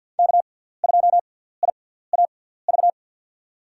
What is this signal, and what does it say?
Morse code, heard as a keyed tone, about 53 wpm, spaced as for 17 wpm, 700 Hz. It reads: K2IAV
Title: USA Call Signs - 5 Characters - 6x Character Spacing - 50wpm